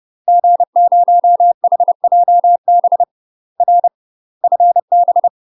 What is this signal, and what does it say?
Morse code, heard as a keyed tone, about 30 wpm, 700 Hz. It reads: G0HJB R FB
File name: UK MASTER QSOs TRAINING / 30WPM / UK MASTER QSOs 0039